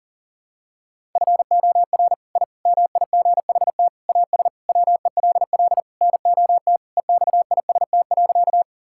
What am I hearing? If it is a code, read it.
Morse code, heard as a keyed tone, about 40 wpm, 700 Hz. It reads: FOR I MIGHT AS WELL NOT EXIST.
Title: Sentences from Top 1000 Words - Review Part 5 - 40wpm